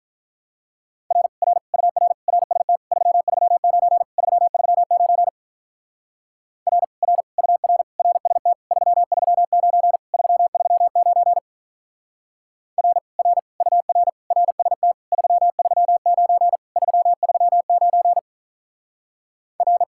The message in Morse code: R R UR RST 339 339 R R UR RST 339 339 R R UR RST 339 339 R

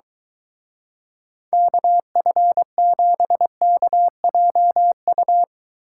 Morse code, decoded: KF7KJU

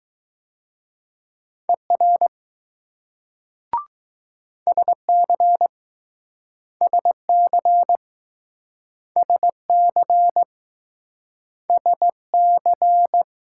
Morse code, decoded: ER SC SC SC SC

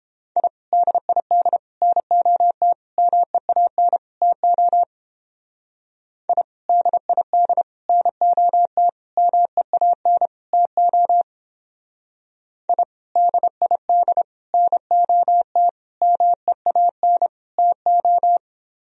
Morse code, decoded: I DID NOT MEAN TO I DID NOT MEAN TO I DID NOT MEAN TO

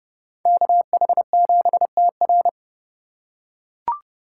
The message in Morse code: KH7TR

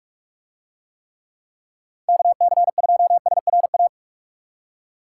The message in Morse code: KC2SRA

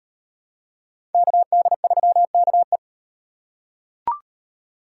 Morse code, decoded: KD3KE

